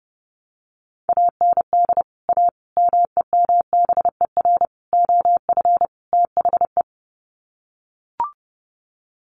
Morse code, heard as a keyed tone, about 30 wpm, 700 Hz. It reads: AND A MEMBER OF THE
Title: Sets of 5 Words 30wpm